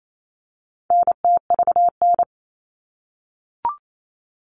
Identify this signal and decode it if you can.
Morse code, heard as a keyed tone, about 28 wpm, 700 Hz. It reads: NTVN
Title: All Cut Numbers - 4-Digit 28wpm